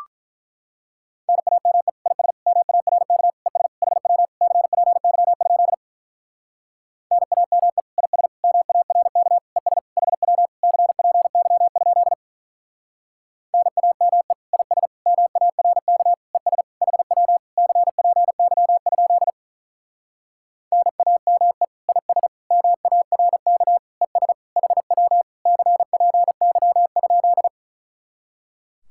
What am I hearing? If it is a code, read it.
Morse code, heard as a keyed tone, about 44 wpm, 700 Hz. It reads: NAME IS MARK ES HW CPY? NAME IS MARK ES HW CPY? NAME IS MARK ES HW CPY? NAME IS MARK ES HW CPY?
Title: Speed-Racing Common QSO Phrases - 35wpm